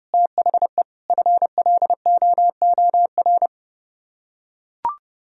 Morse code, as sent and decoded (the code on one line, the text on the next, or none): - .... / ..-. .-.. --- --- .-.
THE FLOOR